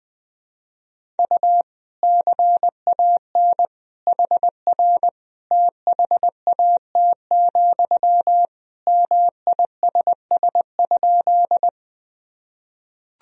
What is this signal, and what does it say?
Morse code, heard as a keyed tone, about 21 wpm, 700 Hz. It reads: U CAN HR THAT, MISS?
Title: Sentences from Top 500 Words and Abbreviations 20wpm